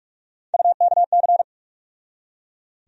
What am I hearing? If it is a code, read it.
Morse code, heard as a keyed tone, about 41 wpm, 700 Hz. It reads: UKC
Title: USA Call Sign Postfixes - 3 Letters 45wpm